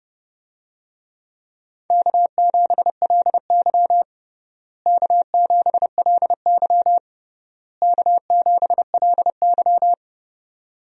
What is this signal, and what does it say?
Morse code, heard as a keyed tone, about 30 wpm, 700 Hz. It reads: K7LY K7LY K7LY
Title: USA Call Signs - 4 Characters - Repeated 3x - 30wpm